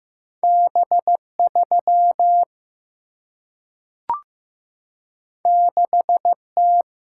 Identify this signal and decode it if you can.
Morse code, heard as a keyed tone, about 15 wpm, 700 Hz. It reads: B3 6T
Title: Single-Letter Single-Number 15wpm